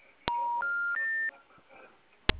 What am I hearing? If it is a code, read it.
Special information tone.
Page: Itapetininga